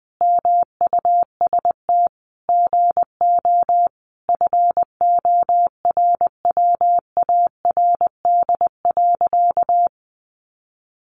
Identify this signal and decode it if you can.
Morse code, heard as a keyed tone, about 20 wpm, 700 Hz. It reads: MUST GO FORWARD.
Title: Sentences from Top 1300 Words 20wpm